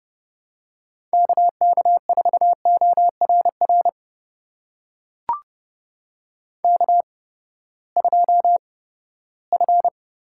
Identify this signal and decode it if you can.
Morse code, heard as a keyed tone, about 30 wpm, spaced as for 17 wpm, 700 Hz. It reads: KK4ORR K2F
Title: USA Call Signs - 8x Character Spacing - 30wpm